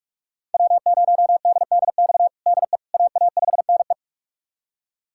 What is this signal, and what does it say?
Morse code, heard as a keyed tone, about 46 wpm, 700 Hz. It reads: W0DDX DE AA5NE
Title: USA Callsign DE Callsign - 5 Characters - 45wpm